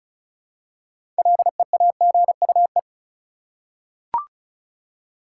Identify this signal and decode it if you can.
Morse code, heard as a keyed tone, about 34 wpm, 700 Hz. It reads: LEAGUE
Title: Six Letter Words - 35wpm